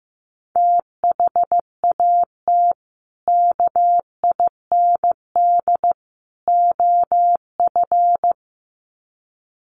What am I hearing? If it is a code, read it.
Morse code, heard as a keyed tone, about 15 wpm, 700 Hz. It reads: THAT KIND OF
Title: Sets of 3 Words 15wpm